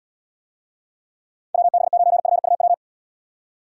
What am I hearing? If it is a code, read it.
Morse code, heard as a keyed tone, about 75 wpm, 700 Hz. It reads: WB9FUW